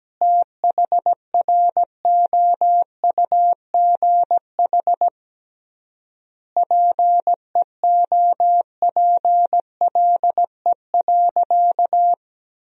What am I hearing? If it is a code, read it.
Morse code, heard as a keyed tone, about 17 wpm, 700 Hz. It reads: THROUGH PEOPLE.
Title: 3x Word-Spacing - Sentences from Top 200 Words - 17wpm